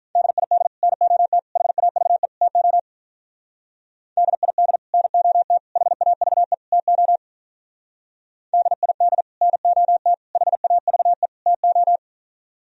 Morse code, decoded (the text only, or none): DID NOT HAVE TO DID NOT HAVE TO DID NOT HAVE TO